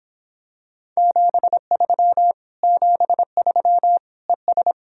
Morse code, heard as a keyed tone, about 26 wpm, 700 Hz. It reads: 73 73 ES